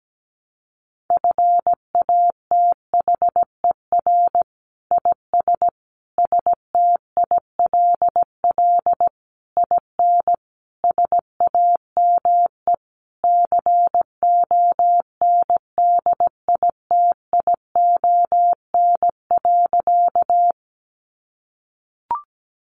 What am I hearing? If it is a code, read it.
Morse code, heard as a keyed tone, about 17 wpm, 700 Hz. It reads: FATHER IS STILL IN SAME CONDITION.
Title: Sentences from Top 1000 Words 17wpm